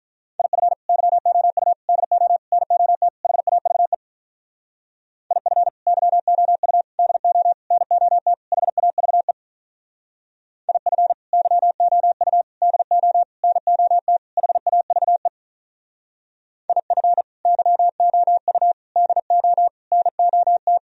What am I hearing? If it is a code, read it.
Morse code, heard as a keyed tone, about 44 wpm, 700 Hz. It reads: IF YOU DO NOT HAVE IF YOU DO NOT HAVE IF YOU DO NOT HAVE IF YOU DO NOT
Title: Speed-Racing - Sets of 5 Words 35wpm